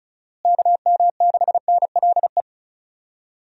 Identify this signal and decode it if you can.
Morse code, heard as a keyed tone, about 36 wpm, 700 Hz. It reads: KM6NLE
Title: USA Call Signs - 6 Characters - Repeated 3x - 35wpm